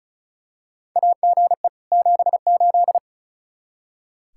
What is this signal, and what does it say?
Morse code, heard as a keyed tone, about 36 wpm, 700 Hz. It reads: AGE 78
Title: Practice QSO - Exchange 3 - More Info - 35wpm